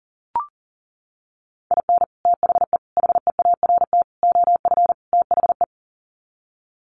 Morse code, heard as a keyed tone, about 40 wpm, 700 Hz. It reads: IN THE HEART OF THE
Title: Sets of 5 Words 40wpm